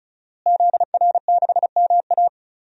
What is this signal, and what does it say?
Morse code, heard as a keyed tone, about 34 wpm, 700 Hz. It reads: ZR6MA
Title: DXCC-100 Call Signs - 5 Characters - Repeated 3x - 35wpm